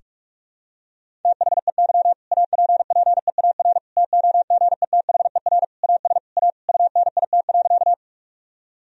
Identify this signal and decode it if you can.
Morse code, heard as a keyed tone, about 41 wpm, 700 Hz. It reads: THEY APPEAR TOGETHER AS A UNIT.
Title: Sentences from Top 1000 Words - Review Part 2 - 45wpm